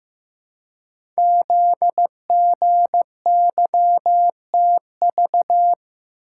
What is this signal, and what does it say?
Morse code, heard as a keyed tone, about 15 wpm, 700 Hz. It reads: ZGYTV
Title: Code Groups - 5 Letters - 15wpm